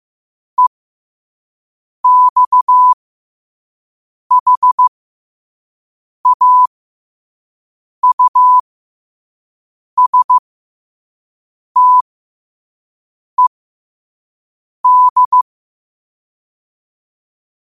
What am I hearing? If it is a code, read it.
Morse code, heard as a keyed tone, about 15 wpm, spaced as for 5 wpm, 1000 Hz. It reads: EXHAUSTED
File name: qotdfiles / 5wpm